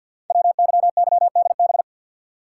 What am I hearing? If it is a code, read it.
Morse code, heard as a keyed tone, about 50 wpm, 700 Hz. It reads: WYYDB
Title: Code Groups - 5 Letters - 50wpm